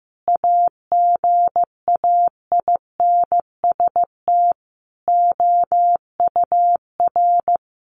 Morse code, heard as a keyed tone, about 15 wpm, 700 Hz. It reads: AGAINST OUR
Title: Sentences from Top 1500 Words 15wpm